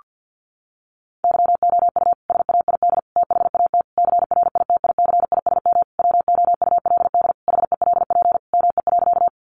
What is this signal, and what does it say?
Morse code, heard as a keyed tone, about 50 wpm, 700 Hz. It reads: YOU SAID THAT CRITICISM WOULD HELP ME.
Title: Sentences from Top 1300 Words 50wpm